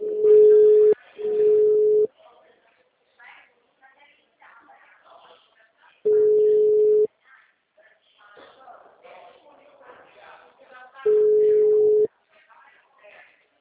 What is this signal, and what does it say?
Ringback tone (Europe, 425 Hz), with speech.